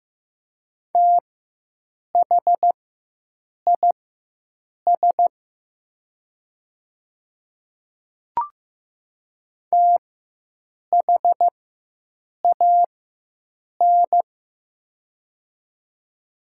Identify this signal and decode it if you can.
Morse code, heard as a keyed tone, about 15 wpm, spaced as for 7 wpm, 700 Hz. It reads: THIS THAN
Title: 4x Character Spacing - Top 100 Words - 15wpm